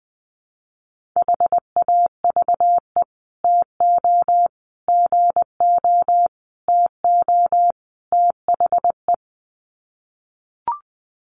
Morse code, heard as a keyed tone, about 20 wpm, 700 Hz. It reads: HAVE TO GO TO THE
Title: Sets of 5 Words Encore 20wpm